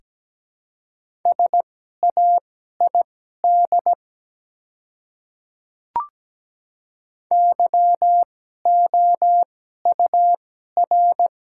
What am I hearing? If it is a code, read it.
Morse code, heard as a keyed tone, about 17 wpm, spaced as for 12 wpm, 700 Hz. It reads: SAID YOUR
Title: 2x Character Spacing - Top 100 Words - 17wpm